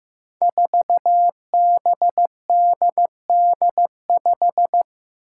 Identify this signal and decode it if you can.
Morse code, heard as a keyed tone, about 15 wpm, 700 Hz. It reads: 4BDD5
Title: Five Challenging Letter-Number Combinations 15wpm